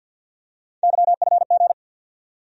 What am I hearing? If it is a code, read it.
Morse code, heard as a keyed tone, about 54 wpm, 700 Hz. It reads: YFG
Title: USA Call Sign Postfixes - 3 Letters 50wpm